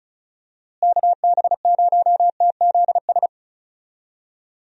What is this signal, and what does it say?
Morse code, heard as a keyed tone, about 36 wpm, 700 Hz. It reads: KB0TZS
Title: USA Call Signs - 6 Characters - Repeated 3x - 35wpm